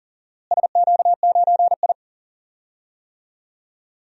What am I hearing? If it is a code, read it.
Morse code, heard as a keyed tone, about 40 wpm, 700 Hz. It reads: SQ9I
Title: DXCC-100 Call Signs - 4 Characters - Repeated 2x - 40wpm